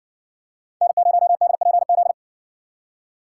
Morse code, heard as a keyed tone, about 57 wpm, 700 Hz. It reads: N0DPZ